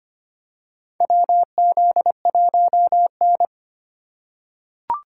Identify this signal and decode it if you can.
Morse code, heard as a keyed tone, about 25 wpm, 700 Hz. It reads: WZ1N
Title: USA Call Signs - 4 Characters - 4x Character Spacing - 25wpm